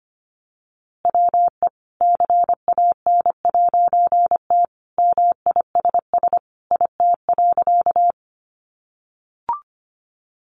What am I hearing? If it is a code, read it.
Morse code, heard as a keyed tone, about 25 wpm, 700 Hz. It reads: WE CAN'T MISS IT.
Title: A Sound Of Thunder 25wpm